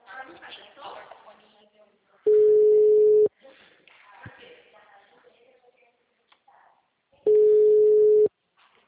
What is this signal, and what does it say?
Ringback tone (Europe, 425 Hz), with speech.
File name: out- -2016- - - .19035